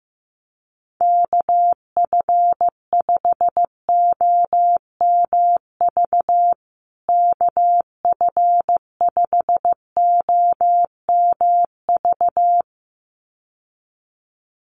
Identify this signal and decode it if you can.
Morse code, heard as a keyed tone, about 15 wpm, 700 Hz. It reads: KF5OMV KF5OMV
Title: USA Call Signs Repeated 2x 15wpm